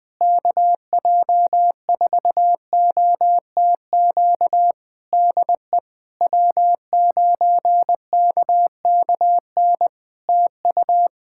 Morse code, heard as a keyed tone, about 20 wpm, 700 Hz. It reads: KJ4OTQ DE W9KKN TU
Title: Practice QSO - Exchange 4 - Ending - 20wpm